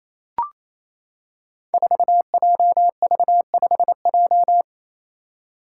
Morse code, heard as a keyed tone, about 28 wpm, 700 Hz. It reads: 4JV5J
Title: Five Challenging Letter-Number Combinations 28wpm